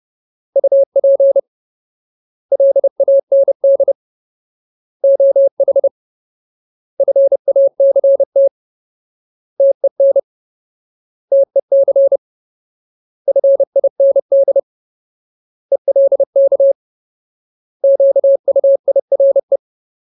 Morse code, heard as a keyed tone, about 30 wpm, 550 Hz. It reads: UP LAND OH FACT TEN TEC FIND ELK QUIRE
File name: Ultimate Word List / 30WPM Ultimate